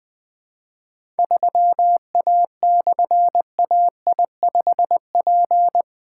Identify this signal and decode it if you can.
Morse code, heard as a keyed tone, about 20 wpm, 700 Hz. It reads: 3A/AI5P
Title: Difficult Call Signs 20wpm